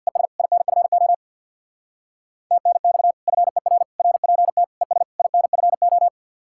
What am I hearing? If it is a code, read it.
Morse code, heard as a keyed tone, about 60 wpm, 700 Hz. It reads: ES INFO TNX FER RPT ES INFO
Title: Speed-Racing Common QSO Phrases Encore - 50wpm